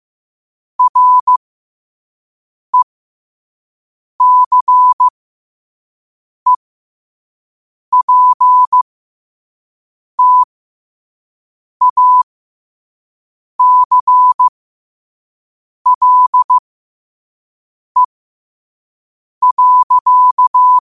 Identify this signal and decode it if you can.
Morse code, heard as a keyed tone, about 15 wpm, spaced as for 5 wpm, 1000 Hz. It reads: RECEPTACLE.
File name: qotdfiles / 5wpm